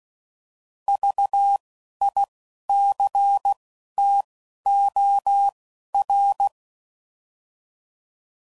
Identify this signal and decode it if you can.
Morse code, heard as a keyed tone, about 16 wpm, 800 Hz. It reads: V I C T O R
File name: Prefixes / By First Letter / Vs